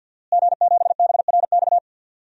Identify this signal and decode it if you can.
Morse code, heard as a keyed tone, about 51 wpm, 700 Hz. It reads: G7BRX
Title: DXCC-100 Call Signs - 5 Characters - Repeated 2x - 50wpm